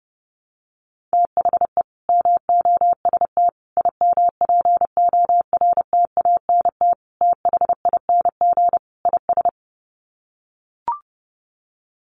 Morse code, heard as a keyed tone, about 30 wpm, 700 Hz. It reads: THE MOST IMPORTANT THING IS
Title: Sets of 5 Words Encore 30wpm